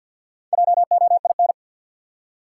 Morse code, heard as a keyed tone, about 53 wpm, 700 Hz. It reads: JOIN
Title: Four Letter Words - 50wpm